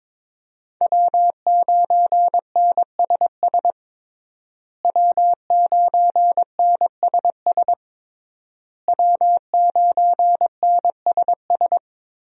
Morse code, heard as a keyed tone, about 22 wpm, 700 Hz. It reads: W9NSS W9NSS W9NSS
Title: USA Call Signs - 5 Characters - Repeated 3x - 22wpm